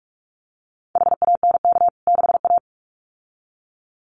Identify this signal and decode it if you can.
Morse code, heard as a keyed tone, about 44 wpm, 700 Hz. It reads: HANK 6A
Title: CWT Exchange 45wpm